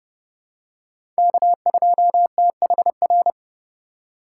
Morse code, heard as a keyed tone, about 32 wpm, 700 Hz. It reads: K2THR
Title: USA Call Signs - 5 Characters - 3x Character Spacing - 30wpm